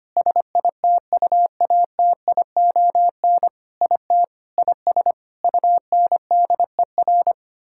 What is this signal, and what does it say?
Morse code, heard as a keyed tone, about 25 wpm, 700 Hz. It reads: SITUATION IT IS UNDER
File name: practice / 25wpm / Sentences with 9-Word Phrases - 25wpm